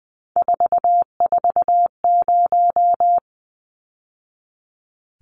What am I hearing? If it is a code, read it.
Morse code, heard as a keyed tone, about 20 wpm, 700 Hz. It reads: 440
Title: Numbers - 3-Digit 20wpm